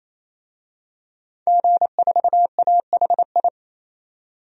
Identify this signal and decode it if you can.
Morse code, heard as a keyed tone, about 28 wpm, 700 Hz. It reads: G4AHI
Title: DXCC-100 Call Signs - 5 Characters - 28wpm